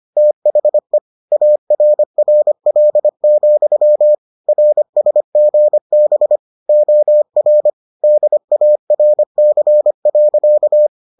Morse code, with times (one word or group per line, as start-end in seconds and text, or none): 0.2-1.0 THE
1.3-4.2 ARRL,
4.5-6.4 RSGB
6.7-7.7 OR
8.0-10.9 DARC.